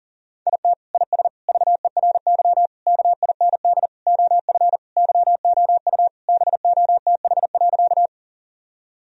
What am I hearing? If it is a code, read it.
Morse code, heard as a keyed tone, about 40 wpm, 700 Hz. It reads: IT IS VERY KIND OF YOU BOTH.
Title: Sentences from Top 1000 Words - Review Part 1 - 40wpm